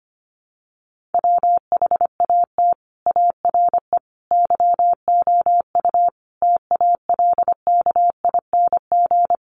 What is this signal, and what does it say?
Morse code, heard as a keyed tone, about 25 wpm, 700 Hz. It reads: WHAT ARE YOU TALKING
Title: Sets of 5 Words 25wpm